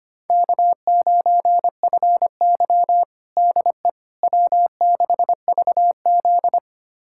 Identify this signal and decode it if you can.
Morse code, heard as a keyed tone, about 25 wpm, 700 Hz. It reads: K9FY DE W6VZ
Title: USA Callsign DE Callsign - 4 Characters - 25wpm